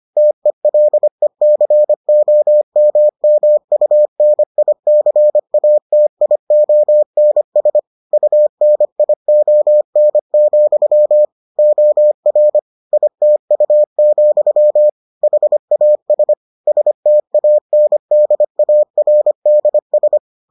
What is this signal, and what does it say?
Morse code, heard as a keyed tone, about 25 wpm, 600 Hz. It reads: TELECOMMUNICATIONS UNION, OR ITU, HAS STANDARDS